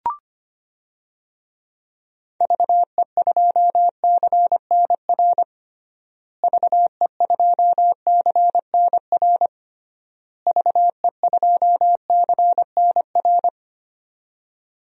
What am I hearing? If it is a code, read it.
Morse code, heard as a keyed tone, about 25 wpm, 700 Hz. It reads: VE2CNR VE2CNR VE2CNR
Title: DXCC-100 Call Signs - 6 Characters - Repeated 3x - 25wpm